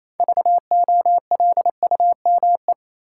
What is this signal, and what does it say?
Morse code, heard as a keyed tone, about 28 wpm, 700 Hz. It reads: VOLUME
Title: Six Letter Words - 28wpm